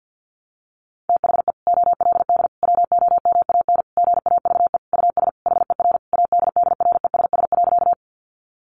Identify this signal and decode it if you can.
Morse code, heard as a keyed tone, about 50 wpm, 700 Hz. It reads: THE OLD WOMAN GAVE US HER ADDRESS.
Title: Sentences from Top 1100 Words 50wpm